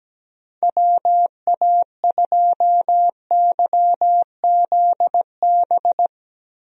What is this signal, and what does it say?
Morse code, heard as a keyed tone, about 17 wpm, 700 Hz. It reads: WA2YZB
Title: USA Call Signs - 6 Characters - Repeated 3x - 17wpm